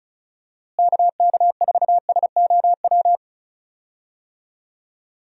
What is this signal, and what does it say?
Morse code, heard as a keyed tone, about 34 wpm, 700 Hz. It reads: KK4SOW
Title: USA Call Signs - 6 Characters - Repeated 2x - 35wpm